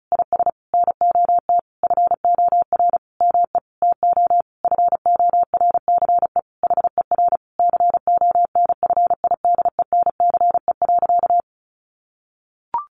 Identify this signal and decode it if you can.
Morse code, heard as a keyed tone, about 34 wpm, 700 Hz. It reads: IS NOT FOR ME TO FORCE HER CONFIDENCE.
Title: Sentences from Top 1500 Words 35wpm